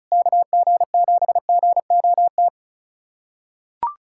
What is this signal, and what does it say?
Morse code, heard as a keyed tone, about 36 wpm, 690 Hz. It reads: KG7GOT
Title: USA Callsign DE Callsign 35wpm